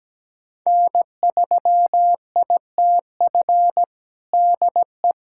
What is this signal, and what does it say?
Morse code, heard as a keyed tone, about 17 wpm, 700 Hz. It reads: N3ITF DE
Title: USA Callsign DE Callsign - 5 Characters - 17wpm